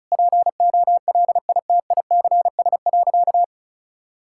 Morse code, heard as a keyed tone, about 34 wpm, 700 Hz. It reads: POLITICS.
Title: Sentences from Top 1000 Words - Review Part 6 - 35wpm